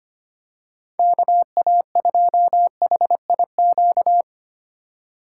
Morse code, heard as a keyed tone, about 25 wpm, 700 Hz. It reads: KA2HIQ
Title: USA Call Signs - 2x Character Spacing - 25wpm